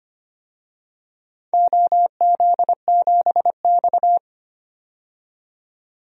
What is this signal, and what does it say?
Morse code, heard as a keyed tone, about 25 wpm, 700 Hz. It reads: OZ7X
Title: DXCC-100 Call Signs - 4 Characters - 25wpm